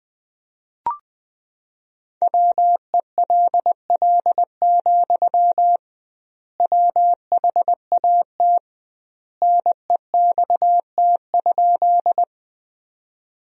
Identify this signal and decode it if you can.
Morse code, heard as a keyed tone, about 21 wpm, 700 Hz. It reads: WELL, WHAT NEXT?
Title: 2x Word-Spacing - Sentences from Top 300 Words - 20wpm